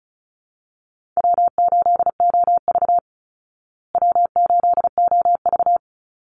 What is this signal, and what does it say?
Morse code, heard as a keyed tone, about 34 wpm, 700 Hz. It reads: W8OV W8OV
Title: Easy Call Signs Repeated 2x - 35wpm